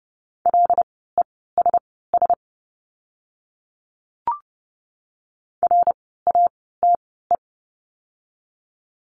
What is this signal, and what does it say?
Morse code, heard as a keyed tone, about 30 wpm, spaced as for 17 wpm, 700 Hz. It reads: LESS RATE
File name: Lesson 16f - Letter c - Words 30wpm